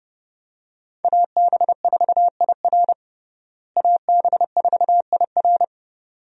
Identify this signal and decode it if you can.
Morse code, heard as a keyed tone, about 30 wpm, 700 Hz. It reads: AB4IR AB4IR
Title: USA Call Signs - 5 Characters - Repeated 2x - 30wpm